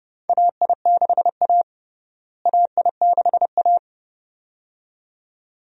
Morse code, heard as a keyed tone, about 30 wpm, 700 Hz. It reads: AI6A AI6A
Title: USA Call Signs - 4 Characters - Repeated 2x - 30wpm